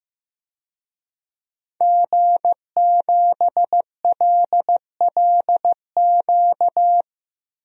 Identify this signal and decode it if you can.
Morse code, heard as a keyed tone, about 15 wpm, 700 Hz. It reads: G7LLQ
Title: DXCC-100 Call Signs - 5 Characters - 15wpm